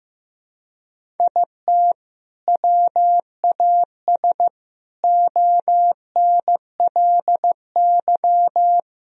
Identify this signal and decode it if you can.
Morse code, heard as a keyed tone, about 15 wpm, 700 Hz. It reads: IT WAS ONLY
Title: Sentences from Top 1000 Words - Review Part 1 - 15wpm